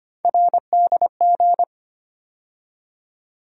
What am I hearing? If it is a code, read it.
Morse code, heard as a keyed tone, about 25 wpm, 700 Hz. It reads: RDG
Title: USA Call Sign Postfixes - 3 Letters 25wpm